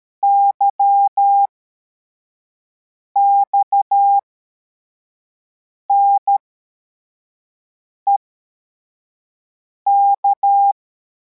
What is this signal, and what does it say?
Morse code, heard as a keyed tone, about 13 wpm, spaced as for 4 wpm, 800 Hz. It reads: YXNEK
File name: formation / cw / Text52